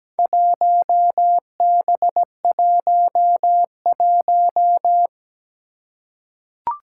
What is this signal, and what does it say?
Morse code, heard as a keyed tone, about 17 wpm, 700 Hz. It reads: 1B11
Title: Four Challenging Letter-Number Combinations 17wpm